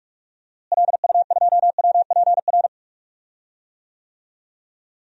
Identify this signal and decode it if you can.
Morse code, heard as a keyed tone, about 42 wpm, 690 Hz. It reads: LU2WPR
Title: DXCC-100 Call Signs - 6 Characters - Repeated 2x - 45wpm